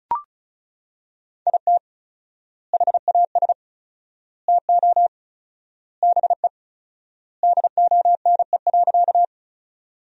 Morse code, heard as a keyed tone, about 36 wpm, 700 Hz. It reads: IT HAS TO BE DONE.
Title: 4x Word-Spacing - Sentences from Top 500 Words - 35wpm